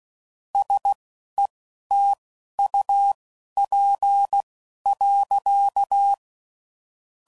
Morse code, heard as a keyed tone, about 16 wpm, spaced as for 11 wpm, 800 Hz. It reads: SETUP.
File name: CW Code Groups @15WPM / QSOs / WD0BGP Spaced